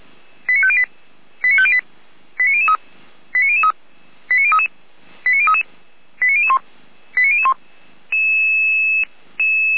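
Five-tone selective call (ZVEI), heard as quick runs of five tones, tones of about 80 ms. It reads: 89508 89508 89003 89003 89033 89033 89001 89001
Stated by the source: ZVEI five-tone selective call received on SDR by fire brigade Feuerwehr Orlamünde.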